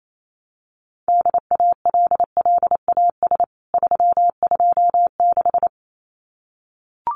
Morse code, heard as a keyed tone, about 28 wpm, 700 Hz. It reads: DALLAS 326
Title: CWT Exchange 28wpm